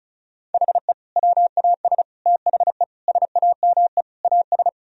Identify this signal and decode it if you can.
Morse code, heard as a keyed tone, about 34 wpm, 700 Hz. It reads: HE WAS THE SAME AS
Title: Sentences from Top 1000 Words - Review Part 2 - 35wpm